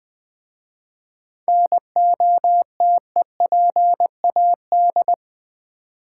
Morse code, heard as a keyed tone, about 20 wpm, 700 Hz. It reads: NOTEPAD
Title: Easy Words with Hints - 20wpm